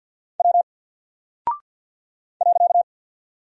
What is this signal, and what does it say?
Morse code, heard as a keyed tone, about 48 wpm, 700 Hz. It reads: W .
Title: Single Character (Letter:Number:Punctuation:Prosign) 50wpm